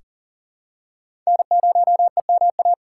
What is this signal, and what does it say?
Morse code, heard as a keyed tone, about 40 wpm, 700 Hz. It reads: N0EMA
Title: USA Call Signs - 5 Characters - 2x Character Spacing - 40wpm